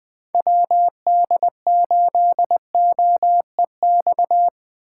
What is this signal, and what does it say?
Morse code, heard as a keyed tone, about 21 wpm, 700 Hz. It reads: WD8OEX